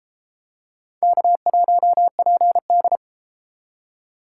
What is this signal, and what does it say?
Morse code, heard as a keyed tone, about 33 wpm, 700 Hz. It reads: K1PD